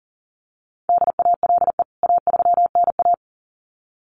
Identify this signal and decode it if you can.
Morse code, heard as a keyed tone, about 40 wpm, 700 Hz. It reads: DALE A3NA
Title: CWT Exchange 40wpm